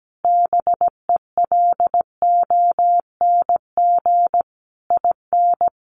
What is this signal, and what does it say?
Morse code, heard as a keyed tone, about 17 wpm, 700 Hz. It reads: BELONG IN
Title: A Sound Of Thunder 17wpm